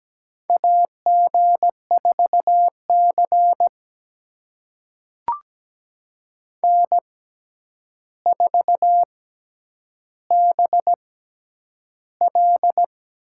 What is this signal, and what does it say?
Morse code, heard as a keyed tone, about 17 wpm, spaced as for 9 wpm, 700 Hz. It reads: AG4C N4BL
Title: USA Call Signs - 4 Characters - 6x Character Spacing - 17wpm